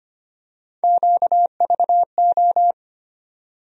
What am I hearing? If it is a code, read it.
Morse code, heard as a keyed tone, about 25 wpm, 700 Hz. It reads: QVO
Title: USA Call Sign Postfixes - 3 Letters 25wpm